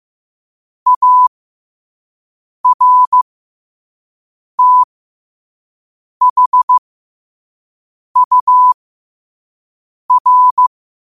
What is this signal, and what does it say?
Morse code, heard as a keyed tone, about 15 wpm, spaced as for 5 wpm, 1000 Hz. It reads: ARTHUR
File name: qotdfiles / 5wpm